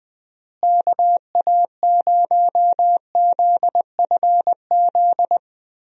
Morse code, heard as a keyed tone, about 20 wpm, 700 Hz. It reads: KA0ZFZ
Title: USA Call Signs - 6 Characters - Repeated 2x - 20wpm